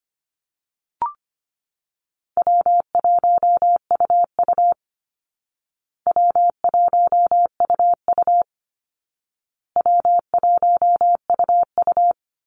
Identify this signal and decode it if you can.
Morse code, heard as a keyed tone, about 25 wpm, 700 Hz. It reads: W1UU W1UU W1UU
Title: Easy Call Signs Repeated 3x - 25wpm